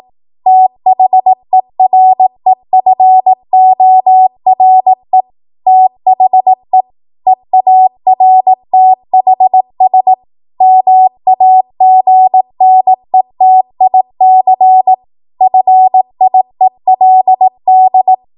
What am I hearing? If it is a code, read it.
Morse code, heard as a keyed tone, about 18 wpm, 750 Hz. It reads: THEREFORE THE EARTHS MAGNETIC FIELD
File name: Morse / 18WPM